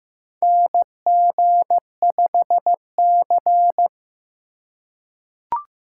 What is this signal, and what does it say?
Morse code, heard as a keyed tone, about 15 wpm, 700 Hz. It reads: NG5C
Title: USA Call Signs - 4x Character Spacing - 15wpm